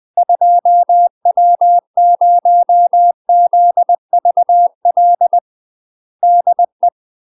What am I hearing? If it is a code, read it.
Morse code, heard as a keyed tone, about 20 wpm, 700 Hz. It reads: 2W0ZVL DE
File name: UK MASTER QSOs TRAINING / 20WPM / UK MASTER QSOs 0002